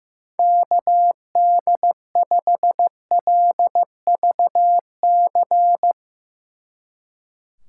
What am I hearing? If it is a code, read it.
Morse code, heard as a keyed tone, about 15 wpm, 700 Hz. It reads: KD5LVC
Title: Speed Racing - USA 2x3 Call Signs 15wpm